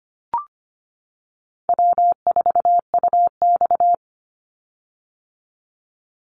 Morse code, heard as a keyed tone, about 25 wpm, 700 Hz. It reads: W4UX
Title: Call Signs - 25wpm